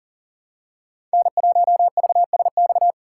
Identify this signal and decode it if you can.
Morse code, heard as a keyed tone, about 40 wpm, 700 Hz. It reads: N1VSX
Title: USA Call Signs - 5 Characters - 4x Character Spacing - 40wpm